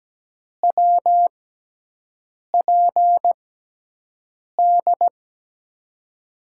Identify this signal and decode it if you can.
Morse code, heard as a keyed tone, about 17 wpm, spaced as for 6 wpm, 700 Hz. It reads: WPD
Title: USA Call Signs - 6 Characters - 6x Character Spacing - 17wpm